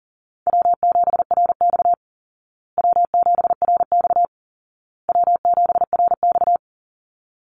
Morse code, heard as a keyed tone, about 40 wpm, 700 Hz. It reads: W7RX W7RX W7RX
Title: Easy Call Signs Repeated 3x - 40wpm